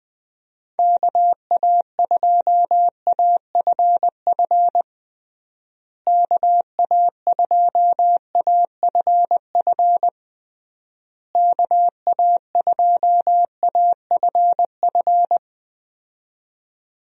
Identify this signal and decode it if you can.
Morse code, heard as a keyed tone, about 20 wpm, 700 Hz. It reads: KA2AFF KA2AFF KA2AFF
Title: USA Call Signs - 6 Characters - Repeated 3x - 20wpm